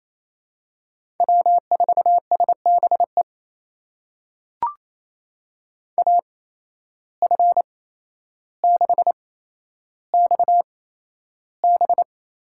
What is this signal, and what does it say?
Morse code, heard as a keyed tone, about 28 wpm, spaced as for 12 wpm, 700 Hz. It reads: W4SBE AF6XB
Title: USA Call Signs - 5 Characters - 8x Character Spacing - 28wpm